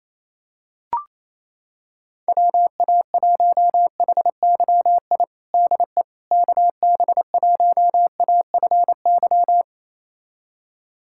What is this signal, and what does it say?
Morse code, heard as a keyed tone, about 28 wpm, 700 Hz. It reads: WA1HYI DE KB1AFY
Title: USA Callsign DE Callsign - 6 Characters - 28wpm